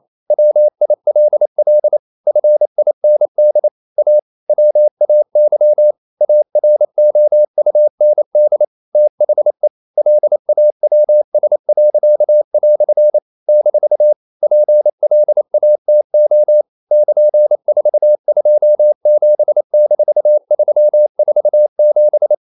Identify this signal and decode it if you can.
Morse code, heard as a keyed tone, about 28 wpm, 600 Hz. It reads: WILL FIND A WAY AROUND THE LAWS." - PLATO <KN>427-347
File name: CW / Practice / EnglishQuotations0128